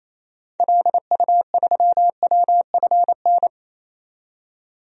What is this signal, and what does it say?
Morse code, heard as a keyed tone, about 28 wpm, 700 Hz. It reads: LU3WFN
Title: DXCC-100 Call Signs - 6 Characters - 28wpm